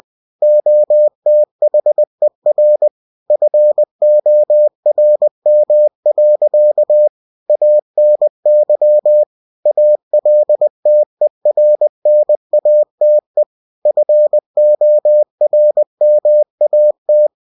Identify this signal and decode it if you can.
Morse code, heard as a keyed tone, about 20 wpm, 600 Hz. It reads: OTHER FORM. ANY ALTERNATE FORMAT
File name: cw-books / winnie-the-pooh / winnie-20wpm-page-0116